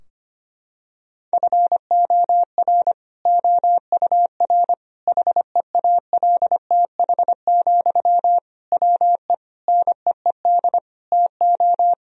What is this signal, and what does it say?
Morse code, heard as a keyed tone, about 25 wpm, 700 Hz. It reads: FOR OUR HEALTH, WE NEED TO
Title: Sentences from Top 1000 Words - Review Part 4 - 25wpm